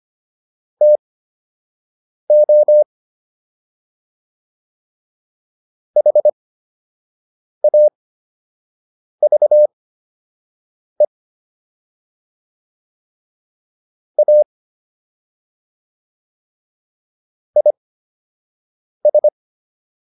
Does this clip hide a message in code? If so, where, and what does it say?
Yes: Morse code from 0.8 to 19.3 s, heard as a keyed tone, about 25 wpm, spaced as for 6 wpm, 600 Hz. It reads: TO HAVE A IS